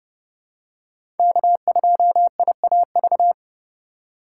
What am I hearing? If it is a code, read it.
Morse code, heard as a keyed tone, about 30 wpm, 700 Hz. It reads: K2IAV
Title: USA Call Signs - 5 Characters - 4x Character Spacing - 30wpm